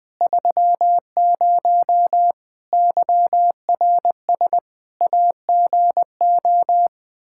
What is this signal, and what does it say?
Morse code, heard as a keyed tone, about 20 wpm, 700 Hz. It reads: 30 YRS AGO
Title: K1USN SST QSO Practice 20wpm